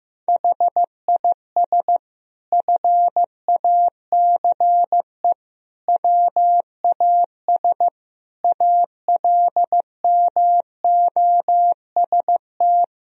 Morse code, heard as a keyed tone, about 16 wpm, 700 Hz. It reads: HIS FACE WAS ALMOST
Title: Sentences from Top 1000 Words - Review Part 4 - 15wpm